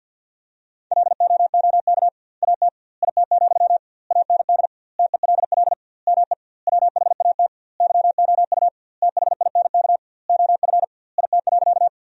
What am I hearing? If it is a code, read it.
Morse code, heard as a keyed tone, about 50 wpm, 700 Hz. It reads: LOOK AT IT, AND TELL ME WHAT YOU THINK OF IT.
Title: 2x Word-Spacing - Sentences from Top 200 Words - 50wpm